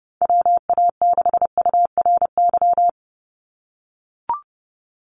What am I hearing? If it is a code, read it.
Morse code, heard as a keyed tone, about 30 wpm, 700 Hz. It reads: WA6URY
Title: Call Signs - 30wpm